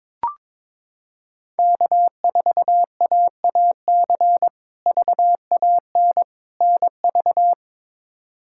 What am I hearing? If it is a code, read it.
Morse code, heard as a keyed tone, about 21 wpm, 700 Hz. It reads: K4AAC VAN NV
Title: K1USN SST QSO Practice - 22wpm